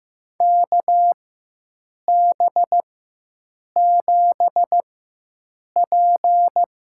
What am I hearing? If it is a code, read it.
Morse code, heard as a keyed tone, about 15 wpm, spaced as for 7 wpm, 700 Hz. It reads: KB7P
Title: USA Call Signs - 4x Character Spacing - 15wpm